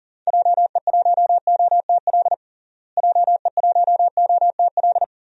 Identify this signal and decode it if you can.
Morse code, heard as a keyed tone, about 40 wpm, 700 Hz. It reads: JE1OTL JE1OTL
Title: DXCC-100 Call Signs - 6 Characters - Repeated 2x - 40wpm